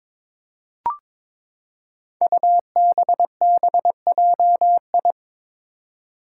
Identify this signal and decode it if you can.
Morse code, heard as a keyed tone, about 22 wpm, 700 Hz. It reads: UBBJI
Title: Five Challenging Letter-Number Combinations 22wpm